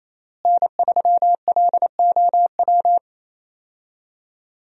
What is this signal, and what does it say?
Morse code, heard as a keyed tone, about 28 wpm, 700 Hz. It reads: N3LOW
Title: USA Call Signs - 5 Characters - Repeated 3x - 28wpm